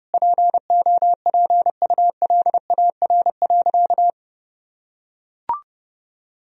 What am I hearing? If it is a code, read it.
Morse code, heard as a keyed tone, about 32 wpm, 700 Hz. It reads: POPULAR.
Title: Sentences from Top 1000 Words - Review Part 4 - 30wpm